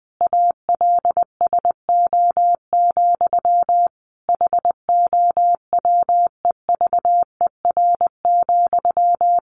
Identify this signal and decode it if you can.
Morse code, heard as a keyed tone, about 20 wpm, 700 Hz. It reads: ALSO, HOWEVER,
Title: Sentences from Top 1400 Words 20wpm